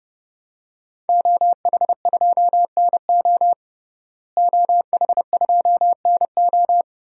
Morse code, heard as a keyed tone, about 30 wpm, 700 Hz. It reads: OH2NO OH2NO
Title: DXCC-100 Call Signs - 5 Characters - Repeated 2x - 30wpm